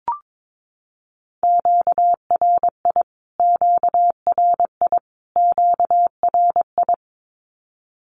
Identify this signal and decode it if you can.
Morse code, heard as a keyed tone, about 22 wpm, 700 Hz. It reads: QRI QRI QRI
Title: Q-Codes Repeated 3x 22wpm